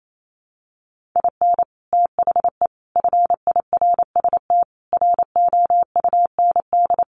Morse code, heard as a keyed tone, about 28 wpm, 700 Hz. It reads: IN THE FIRST ROUND
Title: Sets of 5 Words Encore 28wpm